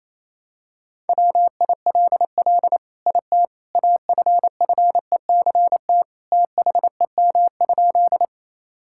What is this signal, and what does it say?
Morse code, heard as a keyed tone, about 28 wpm, 700 Hz. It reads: WILL IT AFFECT THEM?
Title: Sentences from Top 1000 Words - Review Part 6 - 28wpm